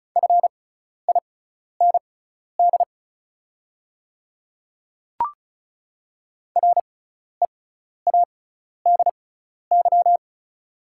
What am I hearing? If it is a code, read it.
Morse code, heard as a keyed tone, about 34 wpm, spaced as for 12 wpm, 690 Hz. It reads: FIND READY
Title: 6x Character Spacing - Top 500 Words - 35wpm